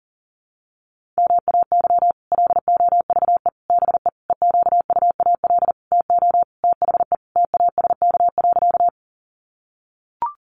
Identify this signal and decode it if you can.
Morse code, heard as a keyed tone, about 40 wpm, 700 Hz. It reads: MAY LOVE BE EQUAL TO THE TASK.
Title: Sentences from Top 1400 Words 40wpm